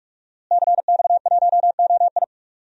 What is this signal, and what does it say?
Morse code, heard as a keyed tone, about 42 wpm, 700 Hz. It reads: CX1OI
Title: DXCC-100 Call Signs - 5 Characters - Repeated 2x - 45wpm